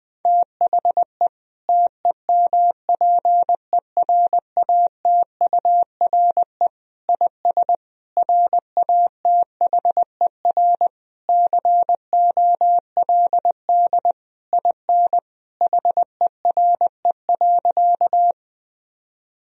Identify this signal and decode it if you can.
Morse code, heard as a keyed tone, about 20 wpm, 700 Hz. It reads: THE TEMPERATURE IS RATHER COLD IN HERE.
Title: Sentences from Top 1000 Words - Review Part 4 - 20wpm